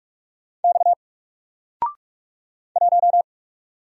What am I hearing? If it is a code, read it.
Morse code, heard as a keyed tone, about 44 wpm, 700 Hz. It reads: X 1
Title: Single Letter-Number Alt 45wpm